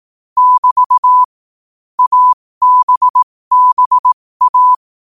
Morse code, heard as a keyed tone, about 18 wpm, spaced as for 15 wpm, 1000 Hz. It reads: <BT> ABBA
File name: qotdfiles / 15wpm